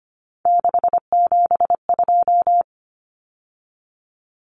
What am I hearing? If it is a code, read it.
Morse code, heard as a keyed tone, about 25 wpm, 700 Hz. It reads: 672
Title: Numbers - 3-Digit 25wpm